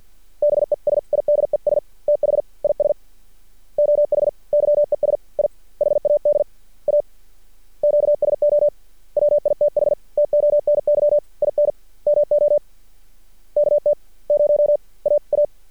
Morse code, heard as a keyed tone, about 50 wpm, spaced as for 42 wpm, 590 Hz. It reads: BESIDES THIS OH YES I HAD A QSO WITH TONY IN MO KT0AA
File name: archived / CRSnet